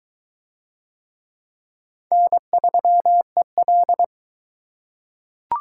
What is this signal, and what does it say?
Morse code, heard as a keyed tone, about 22 wpm, 700 Hz. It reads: N3EL